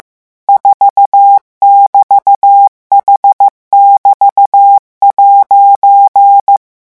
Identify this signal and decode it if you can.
Morse code, heard as a keyed tone, about 15 wpm, 800 Hz. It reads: 4<BT>H<BT>'